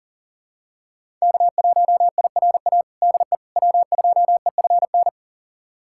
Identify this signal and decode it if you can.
Morse code, heard as a keyed tone, about 40 wpm, 700 Hz. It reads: K1IRA DE W2EFN
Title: USA Callsign DE Callsign - 5 Characters - 40wpm